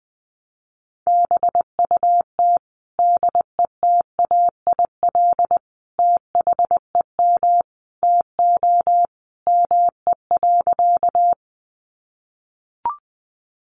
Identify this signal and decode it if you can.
Morse code, heard as a keyed tone, about 20 wpm, 700 Hz. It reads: BUT DETAIL THEM TO ME.
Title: Sentences from Top 1500 Words 20wpm